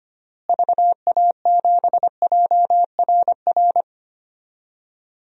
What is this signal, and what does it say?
Morse code, heard as a keyed tone, about 25 wpm, 700 Hz. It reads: VA7JRR
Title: DXCC-100 Call Signs - 6 Characters - Repeated 3x - 25wpm